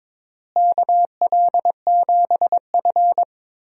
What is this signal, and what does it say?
Morse code, heard as a keyed tone, about 22 wpm, 700 Hz. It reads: KL7F
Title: DXCC-100 Call Signs - 4 Characters - Repeated 3x - 22wpm